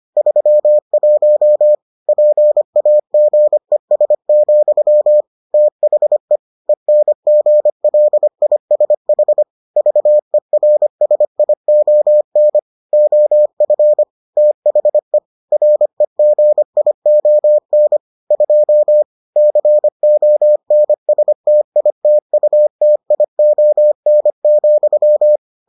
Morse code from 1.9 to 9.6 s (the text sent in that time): PAGES, THE ENGLISH